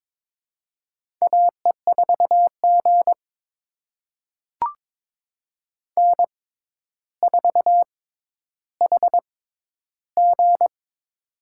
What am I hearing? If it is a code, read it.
Morse code, heard as a keyed tone, about 22 wpm, spaced as for 11 wpm, 700 Hz. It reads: AE4G N4HG
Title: USA Call Signs - 4 Characters - 6x Character Spacing - 22wpm